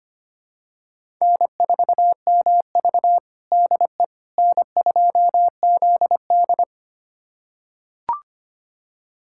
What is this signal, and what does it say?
Morse code, heard as a keyed tone, about 25 wpm, 700 Hz. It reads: N4MV DE N2ZD
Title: USA Callsign DE Callsign - 4 Characters - 25wpm